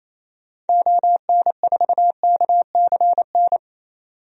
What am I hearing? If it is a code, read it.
Morse code, heard as a keyed tone, about 28 wpm, 700 Hz. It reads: ON4KCN
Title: DXCC-100 Call Signs - 6 Characters - 28wpm